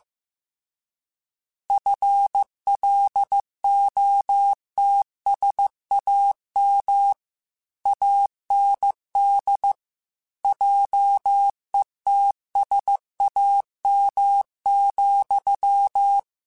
Morse code, heard as a keyed tone, about 15 wpm, 800 Hz. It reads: FLOTSAM AND JETSAM,
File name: Misc / Misc-mp3s / Word Pairs @13